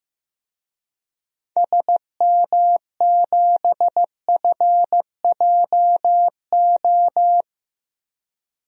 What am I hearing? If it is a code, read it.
Morse code, heard as a keyed tone, about 15 wpm, 700 Hz. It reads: SM7FJO
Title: DXCC-100 Call Signs - 6 Characters - Repeated 2x - 15wpm